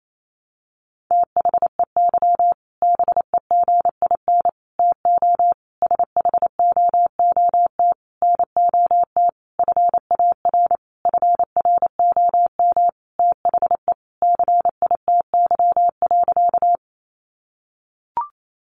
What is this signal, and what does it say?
Morse code, heard as a keyed tone, about 28 wpm, 700 Hz. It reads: THEY BEGIN TO SHOOT NOT FAR FROM THE CITY.
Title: Sentences from Top 1200 Words 28wpm